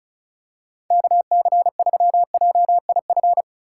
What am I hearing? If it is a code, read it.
Morse code, heard as a keyed tone, about 34 wpm, 700 Hz. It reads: KC3JIF
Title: USA Call Signs - 6 Characters - 4x Character Spacing - 35wpm